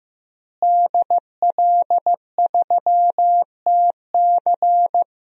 Morse code, heard as a keyed tone, about 15 wpm, 700 Hz. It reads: DL3TC
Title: DXCC-100 Call Signs - 5 Characters - Repeated 3x - 15wpm